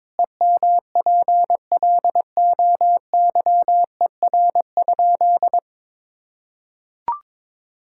Morse code, heard as a keyed tone, about 22 wpm, 700 Hz. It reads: EMPLOYER?
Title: Sentences from Top 1600 Words 22wpm